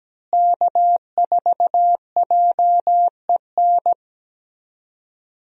Morse code, heard as a keyed tone, about 17 wpm, 700 Hz. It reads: K4JEN
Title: USA Call Signs - 5 Characters - Repeated 3x - 17wpm